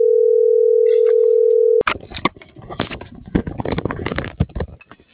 Ringback tone (North America), 440 + 480 Hz.